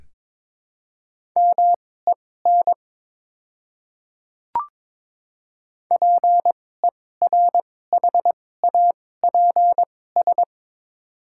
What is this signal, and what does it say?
Morse code, heard as a keyed tone, about 23 wpm, spaced as for 16 wpm, 690 Hz. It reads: MEN PERHAPS
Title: 2x Character Spacing - Top 500 Words - 22wpm